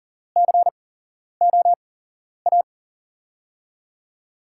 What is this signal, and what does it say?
Morse code, heard as a keyed tone, about 40 wpm, spaced as for 11 wpm, 700 Hz. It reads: COA
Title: USA Call Signs - 6 Characters - 8x Character Spacing - 40wpm